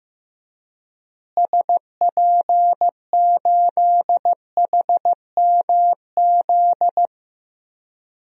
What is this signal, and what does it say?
Morse code, heard as a keyed tone, about 15 wpm, 700 Hz. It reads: SP8HMZ
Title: DXCC-100 Call Signs - 6 Characters - Repeated 3x - 15wpm